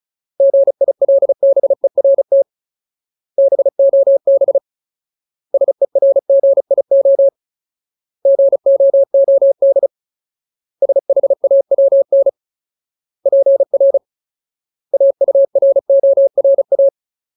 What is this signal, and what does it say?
Morse code, heard as a keyed tone, about 34 wpm, 550 Hz. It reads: GILBERT BOB SERGIO GOOD SHAWN PR AURORA
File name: Ultimate Word List / 35WPM Ultimate